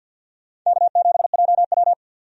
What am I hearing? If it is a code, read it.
Morse code, heard as a keyed tone, about 53 wpm, 700 Hz. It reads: K7JW
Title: USA Call Signs - 4 Characters - 50wpm